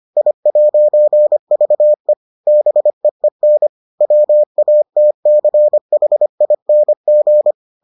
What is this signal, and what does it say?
Morse code, heard as a keyed tone, about 25 wpm, 600 Hz. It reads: I'VE BEEN WATCHING